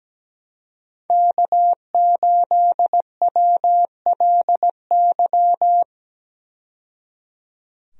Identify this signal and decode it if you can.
Morse code, heard as a keyed tone, about 17 wpm, 700 Hz. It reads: K8WLY
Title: USA Call Signs - 5 Characters - 17wpm